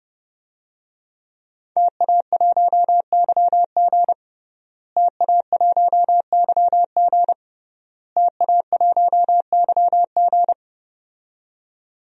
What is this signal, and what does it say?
Morse code, heard as a keyed tone, about 30 wpm, 700 Hz. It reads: TA1YG TA1YG TA1YG
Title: DXCC-100 Call Signs - 5 Characters - Repeated 3x - 30wpm